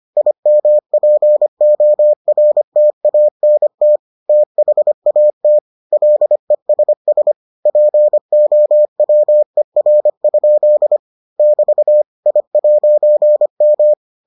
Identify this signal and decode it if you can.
Morse code, heard as a keyed tone, about 25 wpm, 600 Hz. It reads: IMPORTANT THAT LESS POWER? <BT> I'M